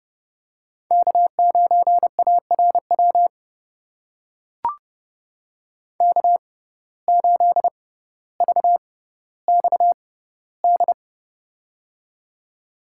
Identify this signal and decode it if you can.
Morse code, heard as a keyed tone, about 30 wpm, spaced as for 15 wpm, 700 Hz. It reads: K9ARW K8VXD
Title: USA Call Signs - 5 Characters - 6x Character Spacing - 30wpm